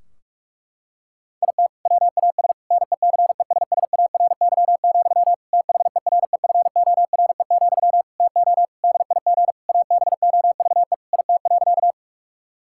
Morse code, heard as a keyed tone, about 43 wpm, 700 Hz. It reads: IT WAS NECESSARY, THEREFORE, TO DIG ABOVE IT.
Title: Sentences from Top 1600 Words 45wpm